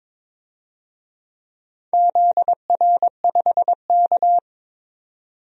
Morse code, heard as a keyed tone, about 22 wpm, 700 Hz. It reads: ZR5K
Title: DXCC-100 Call Signs - 4 Characters - 22wpm